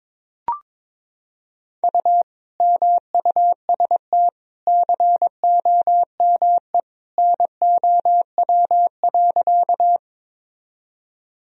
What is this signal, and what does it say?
Morse code, heard as a keyed tone, about 22 wpm, 700 Hz. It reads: U MUST COME NOW.
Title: Sentences from Top 200 Words and Abbreviations 22wpm